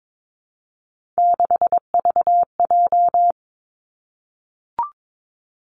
Morse code, heard as a keyed tone, about 22 wpm, 700 Hz. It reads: 6VJ
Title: Three Challenging Letter-Number Combinations 22wpm